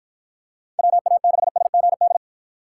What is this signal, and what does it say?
Morse code, heard as a keyed tone, about 53 wpm, 700 Hz. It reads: WA6SGD